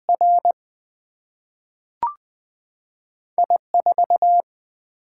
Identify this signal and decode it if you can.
Morse code, heard as a keyed tone, about 20 wpm, 700 Hz. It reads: R I 4
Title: Single-Letter Single-Number 20wpm